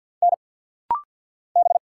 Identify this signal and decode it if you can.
Morse code, heard as a keyed tone, about 50 wpm, 690 Hz. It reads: N B
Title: Single Letters - Rapid-Fire 50wpm